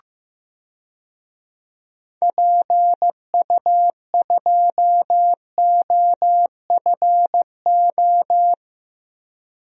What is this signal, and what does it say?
Morse code, heard as a keyed tone, about 15 wpm, 700 Hz. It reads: PU2OFO
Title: DXCC-100 Call Signs - 6 Characters - Repeated 3x - 15wpm